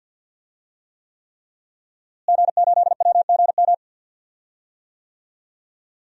Morse code, heard as a keyed tone, about 50 wpm, 700 Hz. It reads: G8WGM
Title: DXCC-100 Call Signs - 5 Characters - Repeated 3x - 50wpm